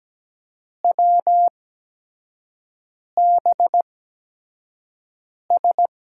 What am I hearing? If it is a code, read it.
Morse code, heard as a keyed tone, about 17 wpm, spaced as for 5 wpm, 700 Hz. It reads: WBS
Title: USA Call Signs - 6 Characters - 8x Character Spacing - 17wpm